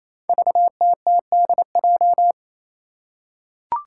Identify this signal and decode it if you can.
Morse code, heard as a keyed tone, about 28 wpm, 700 Hz. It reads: VTTDJ
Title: Five Challenging Letter-Number Combinations 28wpm